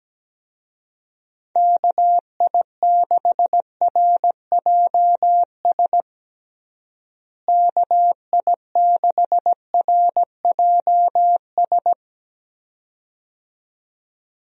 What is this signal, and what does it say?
Morse code, heard as a keyed tone, about 17 wpm, 700 Hz. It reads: KI6RJS KI6RJS
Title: USA Call Signs - 6 Characters - Repeated 2x - 17wpm